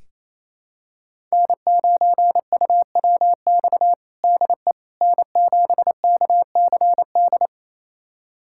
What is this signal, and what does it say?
Morse code, heard as a keyed tone, about 28 wpm, 700 Hz. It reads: N9UWX DE N7KCD
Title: USA Callsign DE Callsign - 5 Characters - 28wpm